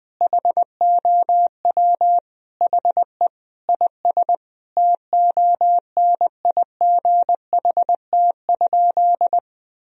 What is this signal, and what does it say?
Morse code, heard as a keyed tone, about 20 wpm, 700 Hz. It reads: HOW HE IS TONIGHT?
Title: Sentences from Top 1000 Words - Review Part 3 - 20wpm